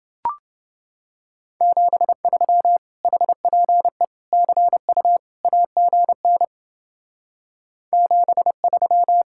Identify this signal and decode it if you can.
Morse code, heard as a keyed tone, about 29 wpm, 700 Hz. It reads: 73 HPE CU AGN 73